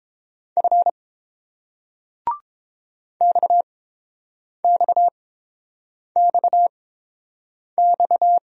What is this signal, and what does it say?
Morse code, heard as a keyed tone, about 30 wpm, spaced as for 7 wpm, 700 Hz. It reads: F XXXX